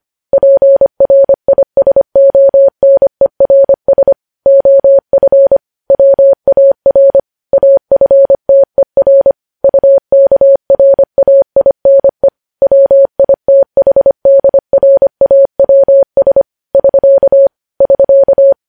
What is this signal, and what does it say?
Morse code, heard as a keyed tone, about 25 wpm, 550 Hz. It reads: PRISONERS OF WAR AFTER UKRAINE WITHDRAWS <SK> <SK>
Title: News Headlines at 25